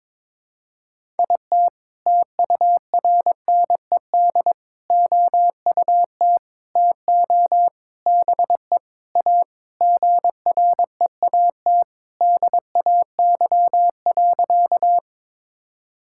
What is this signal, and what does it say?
Morse code, heard as a keyed tone, about 22 wpm, 700 Hz. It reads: IT TURNED OUT TO BE A GREAT DAY.
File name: practice / 22wpm / Sentences with 4-Word Phrases - 22wpm